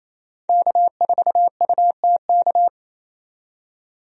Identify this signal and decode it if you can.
Morse code, heard as a keyed tone, about 28 wpm, 700 Hz. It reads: K4UTK
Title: USA Call Signs - 5 Characters - 28wpm